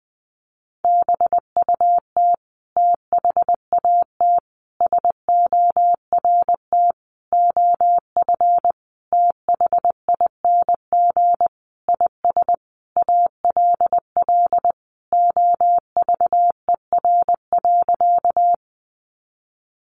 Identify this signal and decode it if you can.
Morse code, heard as a keyed tone, about 20 wpm, 700 Hz. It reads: BUT THAT SORT OF THING IS ALL OVER.
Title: Sentences from Top 1300 Words 20wpm